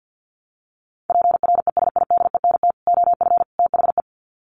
Morse code, heard as a keyed tone, about 50 wpm, 700 Hz. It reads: PRESIDENT OF THE
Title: Sets of 3 Words 50wpm